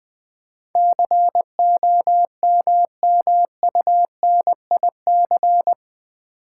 Morse code, heard as a keyed tone, about 20 wpm, 700 Hz. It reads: COMMUNIC
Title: Build A Word - 4 Letters At A Time - 11-Letter Words - 20wpm